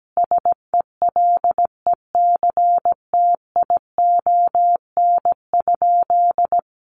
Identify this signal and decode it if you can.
Morse code, heard as a keyed tone, about 17 wpm, 700 Hz. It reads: SELECTION?
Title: Sentences from Top 1400 Words 17wpm